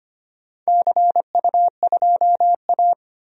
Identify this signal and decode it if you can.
Morse code, heard as a keyed tone, about 25 wpm, 690 Hz. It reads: CU2A
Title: DXCC-100 Call Signs - 4 Characters - 25wpm